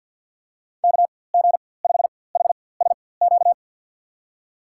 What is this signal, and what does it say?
Morse code, heard as a keyed tone, about 49 wpm, spaced as for 23 wpm, 700 Hz. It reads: KG5HSQ
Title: USA Call Signs - 6 Characters - 4x Character Spacing - 50wpm